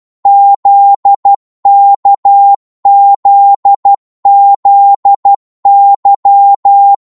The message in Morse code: ZKZZY